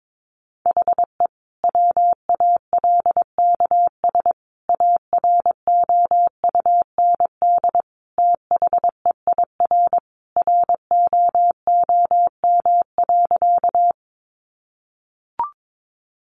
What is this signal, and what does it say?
Morse code, heard as a keyed tone, about 22 wpm, 700 Hz. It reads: HE WALKS AROUND THEIR ROOM.
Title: Easy Phrases 22wpm